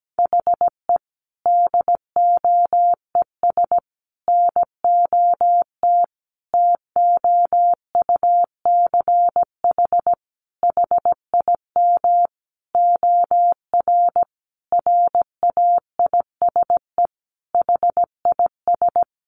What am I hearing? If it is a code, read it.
Morse code, heard as a keyed tone, about 17 wpm, 700 Hz. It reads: HE DOES NOT TOUCH HIM OR RAISE HIS
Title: Sentences from Top 1400 Words 17wpm